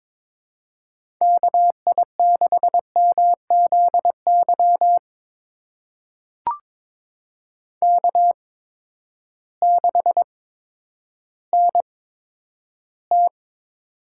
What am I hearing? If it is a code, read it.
Morse code, heard as a keyed tone, about 22 wpm, spaced as for 11 wpm, 700 Hz. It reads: KI6MZY K6NT
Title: USA Call Signs - 8x Character Spacing - 22wpm